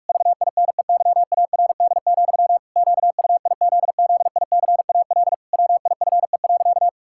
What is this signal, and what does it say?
Morse code, heard as a keyed tone, about 46 wpm, 700 Hz. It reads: VINEYARD, QUIZZICAL WIFE.
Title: Pangram Challenge v2 - 45wpm